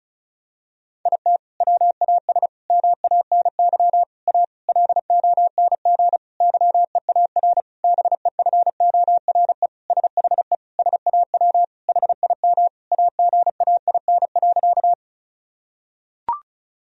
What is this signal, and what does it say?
Morse code, heard as a keyed tone, about 34 wpm, 700 Hz. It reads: IT WAS MANY A LONG YEAR BEFORE SHE SAW HIM AGAIN.
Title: Sentences from Top 1000 Words - Review Part 1 - 35wpm